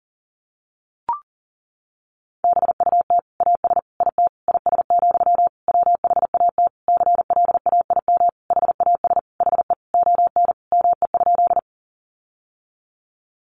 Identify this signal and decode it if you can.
Morse code, heard as a keyed tone, about 40 wpm, 700 Hz. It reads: BUT AS IT IS, WHAT CLAIM HAS HE ON ME?
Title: Sentences from Top 1500 Words 40wpm